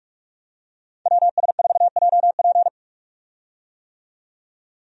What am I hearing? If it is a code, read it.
Morse code, heard as a keyed tone, about 47 wpm, 700 Hz. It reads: WS4JP
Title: USA Call Signs - 5 Characters - 45wpm